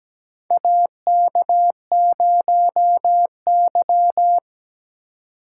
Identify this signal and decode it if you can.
Morse code, heard as a keyed tone, about 17 wpm, 700 Hz. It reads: AK0Y
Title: USA Call Signs - 2x Character Spacing - 17wpm